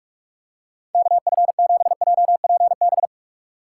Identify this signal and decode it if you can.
Morse code, heard as a keyed tone, about 44 wpm, 700 Hz. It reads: KF7JPB
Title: USA Call Signs - 6 Characters - 6x Character Spacing - 45wpm